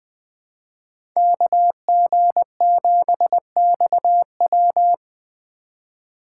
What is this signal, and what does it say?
Morse code, heard as a keyed tone, about 21 wpm, 700 Hz. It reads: KG7XW
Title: USA Call Signs - 5 Characters - 2x Character Spacing - 20wpm